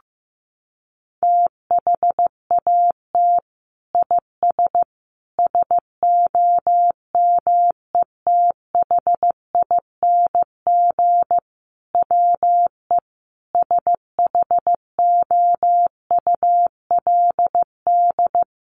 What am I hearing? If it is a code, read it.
Morse code, heard as a keyed tone, about 14 wpm, 700 Hz. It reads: THAT IS SOMETHING WE SHOULD
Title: Sentences from Top 1100 Words 15wpm